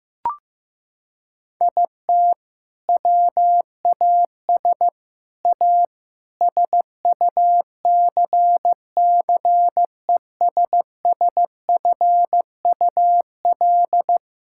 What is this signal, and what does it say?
Morse code, heard as a keyed tone, about 15 wpm, 700 Hz. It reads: IT WAS A SUCCESSFUL
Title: Sentences from Top 1000 Words - Review Part 6 - 15wpm